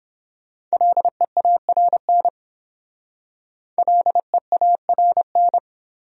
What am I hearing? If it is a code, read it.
Morse code, heard as a keyed tone, about 29 wpm, 700 Hz. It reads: LEARN LEARN